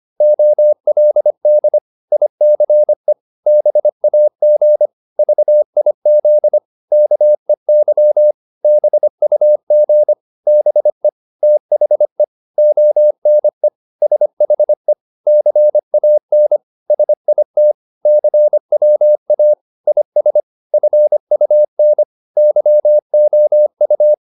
Morse code, heard as a keyed tone, about 25 wpm, 600 Hz. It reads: OLD ICE BAG VIZ KEY BUG BE THE ONE SHE CAN SIT CWA IS FUN YOU